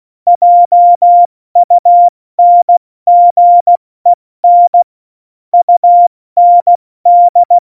Morse code, heard as a keyed tone, about 16 wpm, 700 Hz. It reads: JUNGEN UND
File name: Morsekurs / Witze / Witz 29. Januar 2019, 70 F80